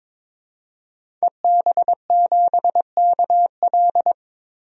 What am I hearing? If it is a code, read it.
Morse code, heard as a keyed tone, about 22 wpm, 700 Hz. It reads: EB7KL
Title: DXCC-100 Call Signs - 5 Characters - Repeated 2x - 22wpm